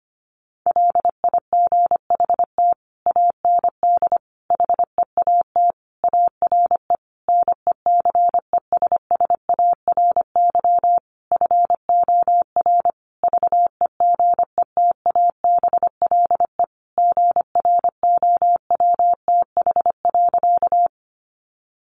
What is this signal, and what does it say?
Morse code, heard as a keyed tone, about 25 wpm, 700 Hz. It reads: LIGHT AND HEAT ARE NECESSARY FOR VEGETABLE GROWTH.
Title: Sentences from Top 1300 Words 25wpm